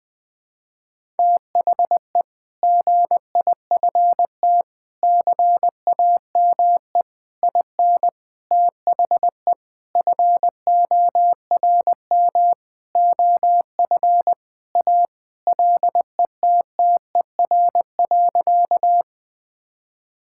Morse code, heard as a keyed tone, about 21 wpm, 700 Hz. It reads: THE GIFT CAME IN THE FORM OF A LETTER.
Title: Sentences with 5-Word Phrases - 20wpm